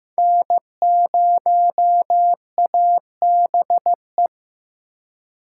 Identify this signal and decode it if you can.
Morse code, heard as a keyed tone, about 15 wpm, 700 Hz. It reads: N0ABE
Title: USA Call Signs - 4x Character Spacing - 15wpm